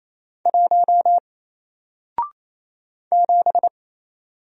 Morse code, heard as a keyed tone, about 28 wpm, 700 Hz. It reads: 1 7
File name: Single-Letter-Number-28wpm